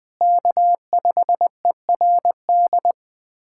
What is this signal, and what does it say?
Morse code, heard as a keyed tone, about 20 wpm, 700 Hz. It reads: K5ERD
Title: USA Callsign DE Callsign - 5 Characters - 20wpm